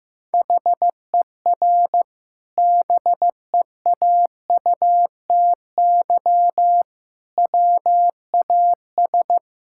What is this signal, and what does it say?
Morse code, heard as a keyed tone, about 15 wpm, 700 Hz. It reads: HER BEAUTY WAS
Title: Sentences from Top 1000 Words - Review Part 6 - 15wpm